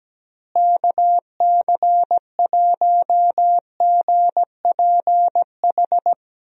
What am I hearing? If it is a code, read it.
Morse code, heard as a keyed tone, about 17 wpm, 700 Hz. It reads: KC1GPH
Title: USA Call Signs - 6 Characters - Repeated 3x - 17wpm